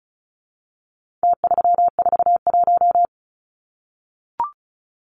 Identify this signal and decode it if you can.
Morse code, heard as a keyed tone, about 34 wpm, 700 Hz. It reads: T341
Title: Numbers - Contest Style - 4-Digit 35wpm